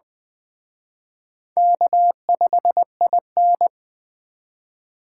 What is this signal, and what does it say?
Morse code, heard as a keyed tone, about 20 wpm, 700 Hz. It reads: K5IN
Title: Speed Racing - USA 1x2 Call Signs 17wpm